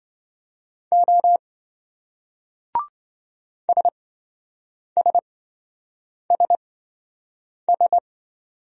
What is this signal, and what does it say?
Morse code, heard as a keyed tone, about 26 wpm, spaced as for 7 wpm, 700 Hz. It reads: O SSSS